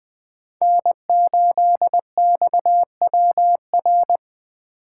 Morse code, heard as a keyed tone, about 20 wpm, 700 Hz. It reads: N8XWR
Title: USA Call Signs - 5 Characters - 3x Character Spacing - 20wpm